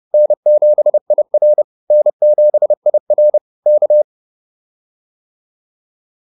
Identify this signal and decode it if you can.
Morse code, heard as a keyed tone, about 30 wpm, 600 Hz. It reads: N7IR N7IR K